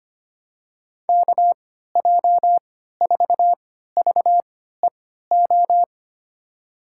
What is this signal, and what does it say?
Morse code, heard as a keyed tone, about 25 wpm, spaced as for 14 wpm, 700 Hz. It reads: KJ4VEO
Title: USA Call Signs - 3x Character Spacing - 25wpm